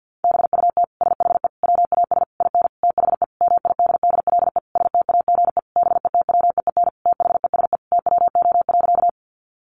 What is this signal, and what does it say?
Morse code, heard as a keyed tone, about 50 wpm, 700 Hz. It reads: BUT SHE WAS IN THE MIDDLE STAGE BETWEEN THESE TWO.
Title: Sentences from Top 1200 Words 50wpm